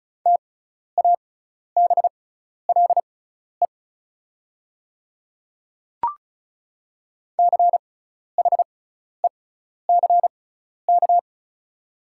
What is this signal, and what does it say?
Morse code, heard as a keyed tone, about 36 wpm, spaced as for 12 wpm, 690 Hz. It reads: TABLE CHECK
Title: 6x Character Spacing - Top 500 Words - 35wpm